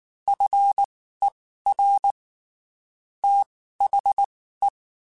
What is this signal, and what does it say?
Morse code, heard as a keyed tone, about 19 wpm, spaced as for 14 wpm, 800 Hz. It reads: FER THE
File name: (spaced) / QSOs Spaced-mp3s / WA7QOX 2 @18 sp